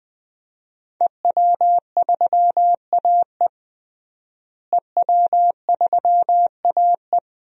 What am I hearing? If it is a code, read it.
Morse code, heard as a keyed tone, about 20 wpm, 700 Hz. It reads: EW3AE EW3AE
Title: DXCC-100 Call Signs - 5 Characters - Repeated 2x - 20wpm